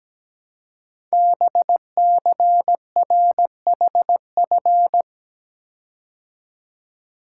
Morse code, heard as a keyed tone, about 17 wpm, 700 Hz. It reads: BCRHF
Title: Code Groups - 5 Letters - 17wpm